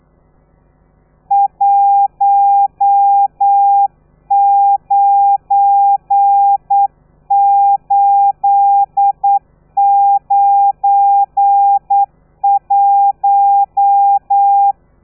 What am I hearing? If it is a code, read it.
Morse code, heard as a keyed tone, about 9 wpm, 800 Hz. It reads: 19891
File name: telegrafia / i2tfj / Lez12 40